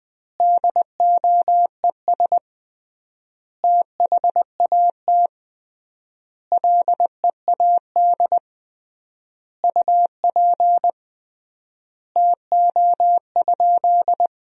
Morse code, heard as a keyed tone, about 20 wpm, 700 Hz. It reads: DOES THAT LEAD UP TO?
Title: 3x Word-Spacing - Sentences from Top 500 Words - 20wpm